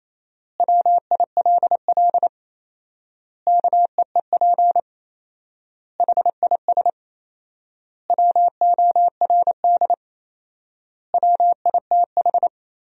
Morse code, heard as a keyed tone, about 28 wpm, 700 Hz. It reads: WILL KEEP HIS WORD WITH
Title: 4x Word-Spacing - Sentences from Top 300 Words - 28wpm